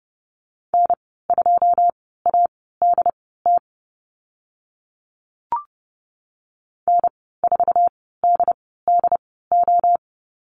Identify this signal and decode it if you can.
Morse code, heard as a keyed tone, about 30 wpm, spaced as for 17 wpm, 700 Hz. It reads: N2ADT N4DDO
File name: Lesson 15g - Number 5 - Call Signs 30wpm